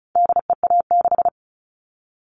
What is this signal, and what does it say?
Morse code, heard as a keyed tone, about 34 wpm, 700 Hz. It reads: DEA6
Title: All Cut Numbers - 4-Digit 35wpm